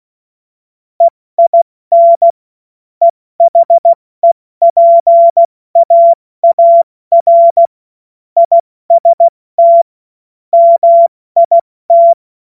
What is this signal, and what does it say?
Morse code, heard as a keyed tone, about 16 wpm, 680 Hz. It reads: EIN EHEPAAR IST MIT
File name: Morsekurs / Witze / Witz 13. Dezember 2022 70 F80